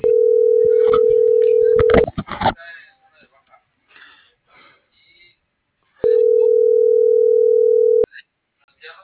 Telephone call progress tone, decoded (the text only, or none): ringback tone